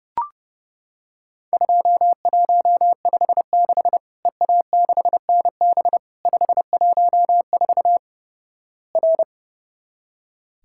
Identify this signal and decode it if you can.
Morse code, heard as a keyed tone, about 29 wpm, 700 Hz. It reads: 2156 EA6NB 514 R
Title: Worked All Europe DX Contest - QTC Exchange - 30wpm